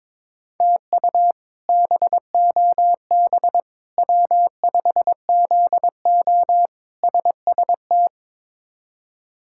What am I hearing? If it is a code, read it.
Morse code, heard as a keyed tone, about 22 wpm, 700 Hz. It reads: TU BOB W5ZO SST
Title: K1USN SST QSO Practice - 22wpm